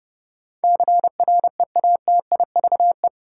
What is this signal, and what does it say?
Morse code, heard as a keyed tone, about 32 wpm, 700 Hz. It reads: CREATIVE
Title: Sentences from Top 1000 Words - Review Part 5 - New Word Drill - 30wpm